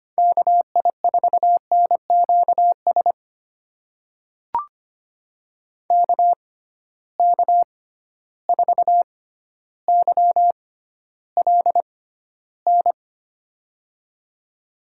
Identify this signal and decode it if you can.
Morse code, heard as a keyed tone, about 25 wpm, spaced as for 13 wpm, 700 Hz. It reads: KI4NQS KK4YLN
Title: USA Call Signs - 6x Character Spacing - 25wpm